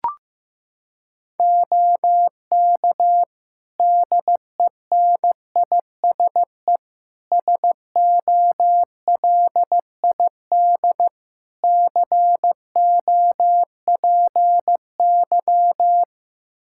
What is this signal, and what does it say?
Morse code, heard as a keyed tone, about 15 wpm, 700 Hz. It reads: OK DENISE SOLID COPY
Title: K1USN SST QSO Practice 15wpm